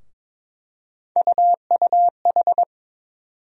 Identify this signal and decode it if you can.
Morse code, heard as a keyed tone, about 22 wpm, 700 Hz. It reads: UUH
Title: USA Call Sign Postfixes - 3 Letters 22wpm